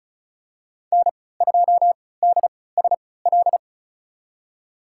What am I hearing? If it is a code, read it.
Morse code, heard as a keyed tone, about 34 wpm, spaced as for 20 wpm, 700 Hz. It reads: N2DSL
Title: USA Call Signs - 5 Characters - 3x Character Spacing - 35wpm